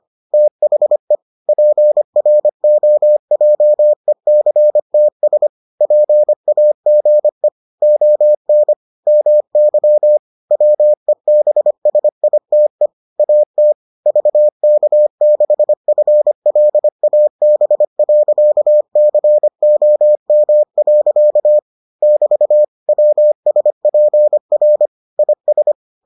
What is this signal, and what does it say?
Morse code, heard as a keyed tone, about 25 wpm, 600 Hz. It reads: THE PROJECTS PAGE ON MY WEBSITE AT VK6FLAB.COM. <BT> WSPR IS